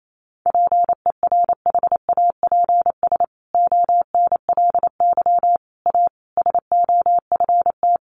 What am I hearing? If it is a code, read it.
Morse code, heard as a keyed tone, about 28 wpm, 700 Hz. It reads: PERHAPS ONLY A SOFT
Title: A Sound Of Thunder 28wpm